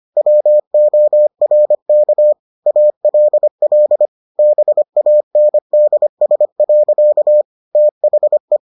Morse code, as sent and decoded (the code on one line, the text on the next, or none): .-- --- .-. -.- / .- .-.. .-.. / -... .- -. -.. ... .-.-.- / - .... .
WORK ALL BANDS. THE